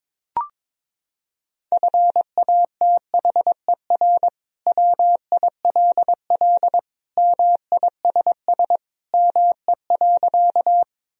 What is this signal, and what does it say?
Morse code, heard as a keyed tone, about 22 wpm, 700 Hz. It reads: FATHER WILL MISS ME.
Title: Sentences from Top 1000 Words - Review Part 3 - 22wpm